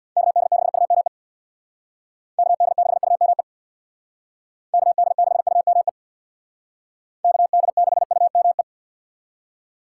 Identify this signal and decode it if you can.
Morse code, heard as a keyed tone, about 63 wpm, 700 Hz. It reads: KD6UME KD6UME KD6UME KD6UME